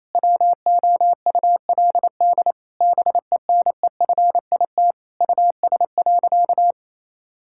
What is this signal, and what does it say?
Morse code, heard as a keyed tone, about 27 wpm, 700 Hz. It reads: WOULD BENEFIT US.
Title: Sentences from Top 1000 Words - Review Part 6 - 28wpm